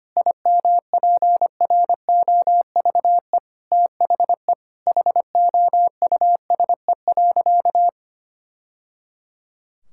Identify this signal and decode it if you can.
Morse code, heard as a keyed tone, about 25 wpm, 700 Hz. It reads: IMPROVE THE HOUSE.
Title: Sentences from Top 1000 Words - Review Part 5 - 25wpm